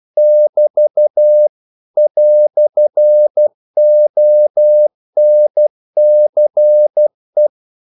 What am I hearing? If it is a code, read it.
Morse code, heard as a keyed tone, about 12 wpm, 600 Hz. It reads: <BT> "ONCE
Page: English quotations in CW (telegraphy - morse)